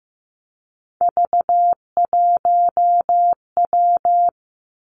Morse code, heard as a keyed tone, about 15 wpm, 700 Hz. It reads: V1W
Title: Three Challenging Letter-Number Combinations 15wpm